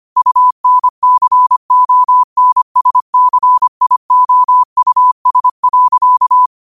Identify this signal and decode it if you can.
Morse code, heard as a keyed tone, about 25 wpm, 1000 Hz. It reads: UNCONSCIOUS.
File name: qotdfiles / 25wpm